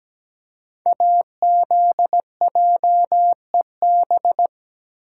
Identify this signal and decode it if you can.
Morse code, heard as a keyed tone, about 17 wpm, 700 Hz. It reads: AZJEB
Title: Code Groups - 5 Letter:Numbers - 17wpm